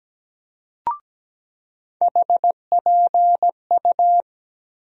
Morse code, heard as a keyed tone, about 17 wpm, 700 Hz. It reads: HPU
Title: USA Call Sign Postfixes - 3 Letters 17wpm